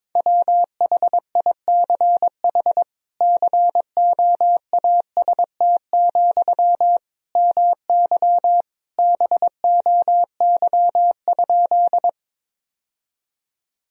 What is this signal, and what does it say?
Morse code, heard as a keyed tone, about 22 wpm, 700 Hz. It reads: WHICH COAST, MY BOY?
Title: Sentences from Top 1000 Words - Review Part 4 - 22wpm